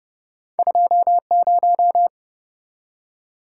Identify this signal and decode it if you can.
Morse code, heard as a keyed tone, about 30 wpm, 700 Hz. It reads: 2 0